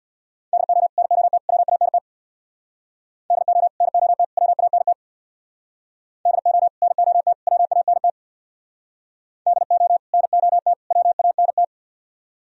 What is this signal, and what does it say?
Morse code, heard as a keyed tone, about 64 wpm, 700 Hz. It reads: DO NOT WANT DO NOT WANT DO NOT WANT DO NOT WANT